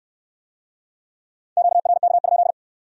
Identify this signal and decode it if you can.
Morse code, heard as a keyed tone, about 67 wpm, 700 Hz. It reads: QRK?